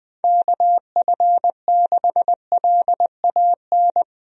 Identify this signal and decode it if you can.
Morse code, heard as a keyed tone, about 20 wpm, 700 Hz. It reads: KF6LAN
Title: USA Call Signs - 6 Characters - 20wpm